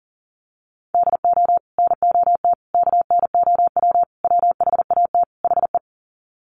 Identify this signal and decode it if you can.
Morse code, heard as a keyed tone, about 40 wpm, 700 Hz. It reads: DO NOT KNOW WHAT HE
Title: Sets of 5 Words Encore2 40wpm